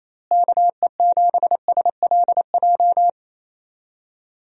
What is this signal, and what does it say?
Morse code, heard as a keyed tone, about 28 wpm, 700 Hz. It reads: KE7SLJ
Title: USA Call Signs - 6 Characters - Repeated 3x - 28wpm